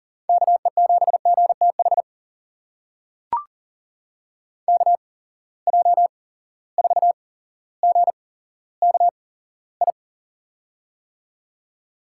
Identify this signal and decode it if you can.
Morse code, heard as a keyed tone, about 40 wpm, spaced as for 17 wpm, 700 Hz. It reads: KE7GTH KJ4GKI
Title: USA Call Signs - 8x Character Spacing - 40wpm